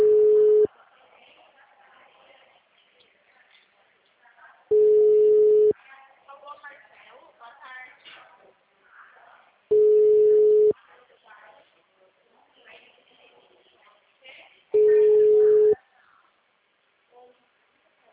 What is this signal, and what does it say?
Ringback tone (Europe, 425 Hz), with speech.